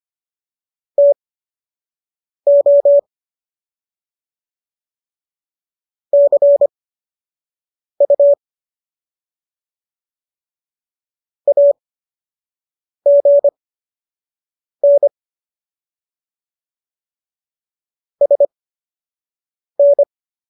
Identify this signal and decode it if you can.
Morse code, heard as a keyed tone, about 25 wpm, spaced as for 6 wpm, 570 Hz. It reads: TO CU AGN SN
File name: qso012 06